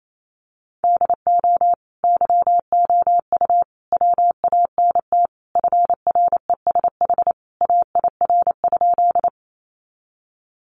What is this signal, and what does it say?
Morse code, heard as a keyed tone, about 28 wpm, 700 Hz. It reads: DO YOU WANT FRESH AIR?
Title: Sentences from Top 1200 Words 28wpm